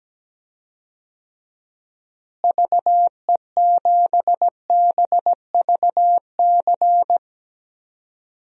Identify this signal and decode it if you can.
Morse code, heard as a keyed tone, about 17 wpm, 700 Hz. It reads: VE7BVC
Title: DXCC-100 Call Signs - 6 Characters - Repeated 2x - 17wpm